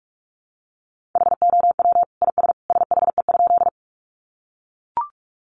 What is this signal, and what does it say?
Morse code, heard as a keyed tone, about 44 wpm, 700 Hz. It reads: HOW IS SHE?
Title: Sentences from Top 100 Words 45wpm